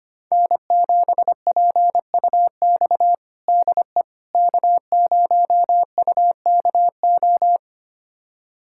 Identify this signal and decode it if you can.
Morse code, heard as a keyed tone, about 25 wpm, 700 Hz. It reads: N7PUX DE K0UKO
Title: USA Callsign DE Callsign - 5 Characters - 25wpm